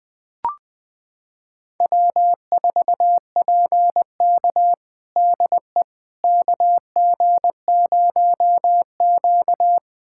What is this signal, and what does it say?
Morse code, heard as a keyed tone, about 20 wpm, 700 Hz. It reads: W4PK DE KG0Q
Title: USA Callsign DE Callsign - 4 Characters - 20wpm